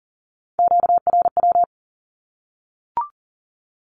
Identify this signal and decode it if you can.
Morse code, heard as a keyed tone, about 40 wpm, 700 Hz. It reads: QRW
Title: Q-Codes 40wpm